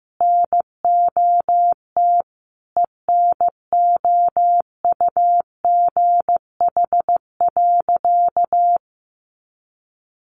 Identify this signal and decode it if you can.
Morse code, heard as a keyed tone, about 15 wpm, 700 Hz. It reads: NOT ENOUGH.
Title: Sentences from Top 600 Words 15wpm